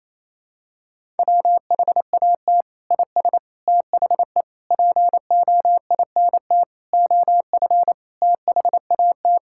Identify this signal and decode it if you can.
Morse code, heard as a keyed tone, about 28 wpm, 700 Hz. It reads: WHAT IS THE POINT OF THAT
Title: Sentences from Top 1000 Words - Review Part 5 - 28wpm